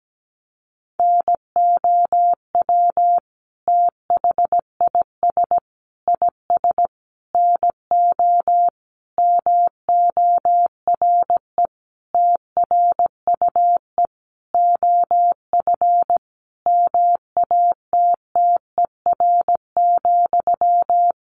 Morse code, heard as a keyed tone, about 17 wpm, 700 Hz. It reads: NOW THIS IS NO MORE TRUE OF MATTER,
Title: Sentences from Top 1400 Words 17wpm